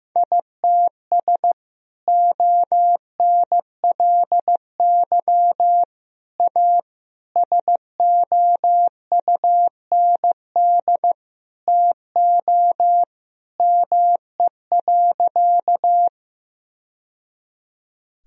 Morse code, heard as a keyed tone, about 15 wpm, 700 Hz. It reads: ITS ONLY A SOUND TO ME.
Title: Sentences from Top 1000 Words - Review Part 1 - 15wpm